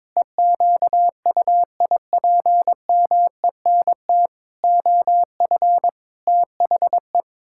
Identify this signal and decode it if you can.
Morse code, heard as a keyed tone, about 22 wpm, 700 Hz. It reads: EQUIPMENT OF THE
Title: Pangram Challenge v2 - 22wpm